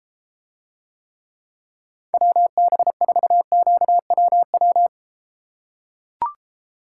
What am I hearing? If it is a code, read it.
Morse code, heard as a keyed tone, about 33 wpm, 700 Hz. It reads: WB4QWW